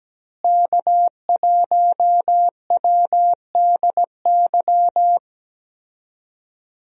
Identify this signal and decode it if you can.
Morse code, heard as a keyed tone, about 17 wpm, 700 Hz. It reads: K1WDY
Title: USA Call Signs - 5 Characters - Repeated 2x - 17wpm